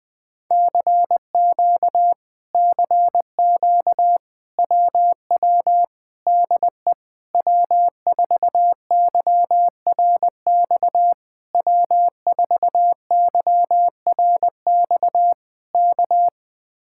Morse code, heard as a keyed tone, about 21 wpm, 700 Hz. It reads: CQ CQ WW DE W4YRX W4YRX K
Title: Speed-Racing Common QSO Phrases - 20wpm